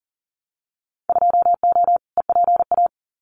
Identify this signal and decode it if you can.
Morse code, heard as a keyed tone, about 40 wpm, 700 Hz. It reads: 2O EPA
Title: Winter Field Day Exchanges Encore 40wpm